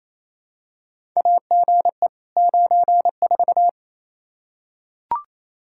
Morse code, heard as a keyed tone, about 27 wpm, 700 Hz. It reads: AGE 94
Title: Practice QSO - Exchange 3 - More Info - 28wpm